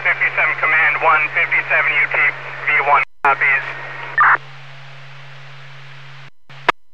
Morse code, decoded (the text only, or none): none